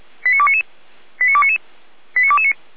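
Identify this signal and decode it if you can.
Five-tone selective call (ZVEI), heard as quick runs of five tones, tones of about 70 ms. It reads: 89299 89299 89229
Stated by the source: ZVEI five-tone selective call received on SDR, from the Feuerwehr Orlamuende audio archive.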